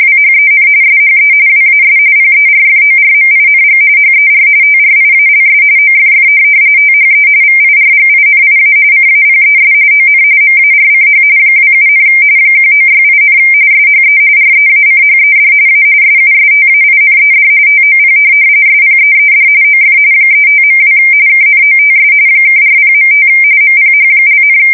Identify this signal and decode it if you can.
RTTY, about 45.45 baud. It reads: WHEN SHE WAS PERFORMING HER TRIALS OFF THE VIRGINIA CAPES ON MARCH 27, 1968, SHE REACHED A SUSTAINED TOP SPEED OF 35.2 KNOTS (40.5 MPH),
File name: fsk170 / USS-New-Jersey